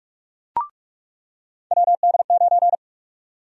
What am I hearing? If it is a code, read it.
Morse code, heard as a keyed tone, about 46 wpm, 700 Hz. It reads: WD9
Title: USA Call Sign Prefixes - 45wpm